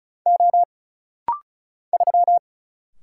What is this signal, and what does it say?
Morse code, heard as a keyed tone, about 36 wpm, 700 Hz. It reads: O 3
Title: Single Letter-Number Alt - Rapid-Fire 35wpm